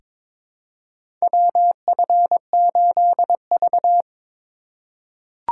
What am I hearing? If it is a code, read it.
Morse code, heard as a keyed tone, about 22 wpm, 700 Hz. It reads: WF8V
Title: USA Call Signs - 4 Characters - 6x Character Spacing - 22wpm